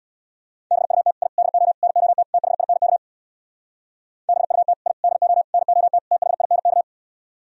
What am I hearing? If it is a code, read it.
Morse code, heard as a keyed tone, about 64 wpm, 700 Hz. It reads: BUT I DO NOT THINK BUT I DO NOT THINK